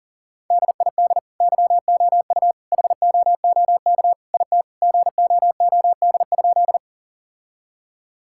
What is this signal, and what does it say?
Morse code, heard as a keyed tone, about 40 wpm, 700 Hz. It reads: DID YOU HOOK IT GOOD?
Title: Sentences from Top 1600 Words 40wpm